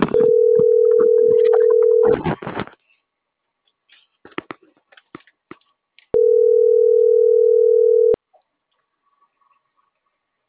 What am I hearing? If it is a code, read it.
Ringback tone (North America), 440 + 480 Hz.